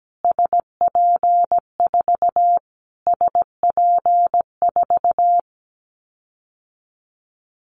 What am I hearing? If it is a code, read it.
Morse code, heard as a keyed tone, about 17 wpm, 700 Hz. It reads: SP4 SP4
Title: Callsign Prefixes for Top 25 DXCC - Repeated 3x - 17wpm